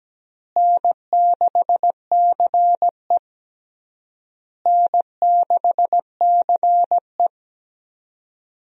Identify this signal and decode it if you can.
Morse code, heard as a keyed tone, about 17 wpm, 700 Hz. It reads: N6CE N6CE
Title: USA Call Signs - 4 Characters - Repeated 3x - 17wpm